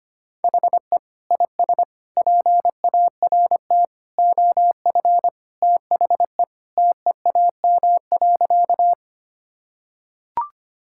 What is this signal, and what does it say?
Morse code, heard as a keyed tone, about 25 wpm, 700 Hz. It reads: HE IS PART OF THE TEAM.
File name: practice / 25wpm / Sentences with 3-Word Phrases - 25wpm